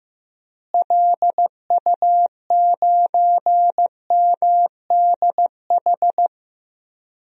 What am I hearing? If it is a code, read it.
Morse code, heard as a keyed tone, about 15 wpm, 700 Hz. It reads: LU9MDH
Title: DXCC-100 Call Signs - 6 Characters - Repeated 2x - 15wpm